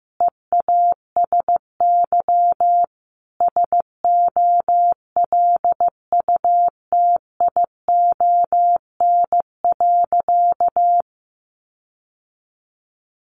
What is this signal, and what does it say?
Morse code, heard as a keyed tone, about 15 wpm, 700 Hz. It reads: EASY SOLUTION.
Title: Sentences from Top 1300 Words 15wpm